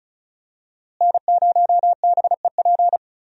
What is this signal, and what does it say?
Morse code, heard as a keyed tone, about 36 wpm, 700 Hz. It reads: N0BEP
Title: USA Call Signs - 5 Characters - Repeated 3x - 35wpm